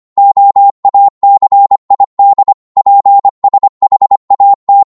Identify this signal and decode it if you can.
Morse code, heard as a keyed tone, about 25 wpm, 800 Hz. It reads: OACID PSHAT
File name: Morse / ltr-25wpm